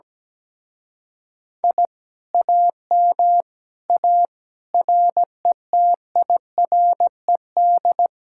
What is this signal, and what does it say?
Morse code, heard as a keyed tone, about 17 wpm, 700 Hz. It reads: I AM A RETIRED
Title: Speed-Racing Common QSO Phrases Encore - 17wpm